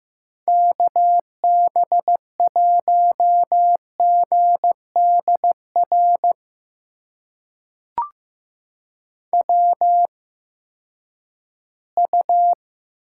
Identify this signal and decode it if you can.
Morse code, heard as a keyed tone, about 15 wpm, spaced as for 10 wpm, 700 Hz. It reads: KB1GDR WU
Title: USA Call Signs - 8x Character Spacing - 15wpm